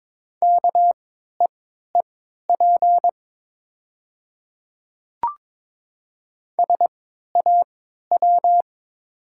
Morse code, heard as a keyed tone, about 22 wpm, spaced as for 12 wpm, 690 Hz. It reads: KEEP SAW
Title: 3x Character Spacing - Top 500 Words - 22wpm